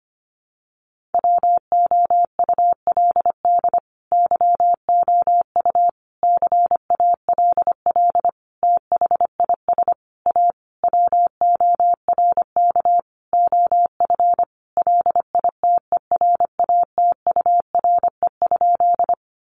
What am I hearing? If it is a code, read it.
Morse code, heard as a keyed tone, about 25 wpm, 700 Hz. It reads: WOULD YOU CALL THIS A WORK OF LITERATURE?
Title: Sentences from Top 600 Words 25wpm